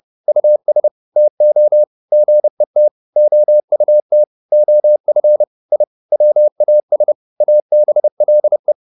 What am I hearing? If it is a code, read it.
Morse code, heard as a keyed tone, about 30 wpm, 600 Hz. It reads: US TO GET OUT OF I WAS ABLE